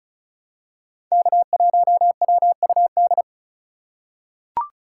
Morse code, heard as a keyed tone, about 36 wpm, 700 Hz. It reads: K1WUD
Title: USA Call Signs - 5 Characters - 4x Character Spacing - 35wpm